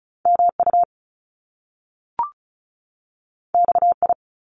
Morse code, heard as a keyed tone, about 36 wpm, 700 Hz. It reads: MU XI
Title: Top Two Letter Combinations in Words 35wpm